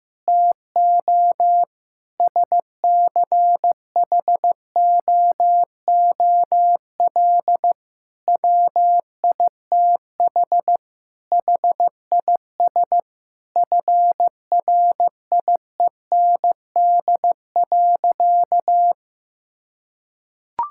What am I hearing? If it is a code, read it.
Morse code, heard as a keyed tone, about 15 wpm, 700 Hz. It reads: TO SCHOOL WITH HIS FRIEND.
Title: Sentences from Top 1600 Words 15wpm